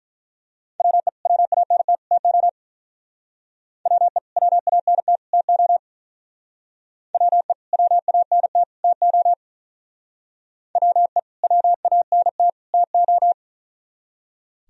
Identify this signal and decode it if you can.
Morse code, heard as a keyed tone, about 43 wpm, 700 Hz. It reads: WE WANT TO WE WANT TO WE WANT TO WE WANT TO
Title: Speed-Racing - Sets of 3 Words 35wpm